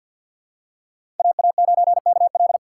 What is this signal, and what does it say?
Morse code, heard as a keyed tone, about 48 wpm, 700 Hz. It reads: AA9KL
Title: USA Call Signs - 5 Characters - 3x Character Spacing - 50wpm